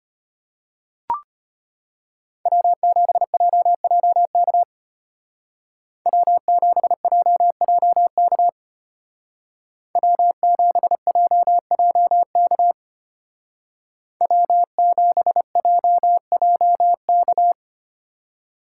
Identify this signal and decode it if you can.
Morse code, heard as a keyed tone, about 30 wpm, 700 Hz. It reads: W7JJK W7JJK W7JJK W7JJK
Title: Speed Racing - USA 1x3 Call Signs 25wpm